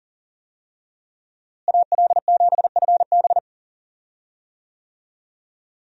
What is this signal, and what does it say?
Morse code, heard as a keyed tone, about 40 wpm, 700 Hz. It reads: AL7FB
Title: USA Call Signs - 5 Characters - Repeated 2x - 40wpm